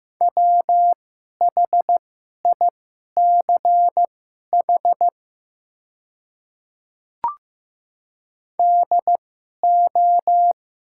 Morse code, heard as a keyed tone, about 15 wpm, spaced as for 11 wpm, 700 Hz. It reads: WHICH DO
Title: 2x Character Spacing - Top 100 Words - 15wpm